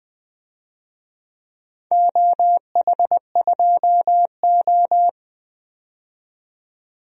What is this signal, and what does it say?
Morse code, heard as a keyed tone, about 20 wpm, 700 Hz. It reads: OH2O
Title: DXCC-100 Call Signs - 4 Characters - 20wpm